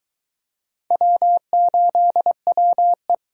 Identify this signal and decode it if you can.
Morse code, heard as a keyed tone, about 24 wpm, 700 Hz. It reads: W8WE